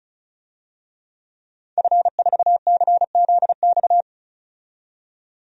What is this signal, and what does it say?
Morse code, heard as a keyed tone, about 36 wpm, 700 Hz. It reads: F4CZX
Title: DXCC-100 Call Signs - 5 Characters - 35wpm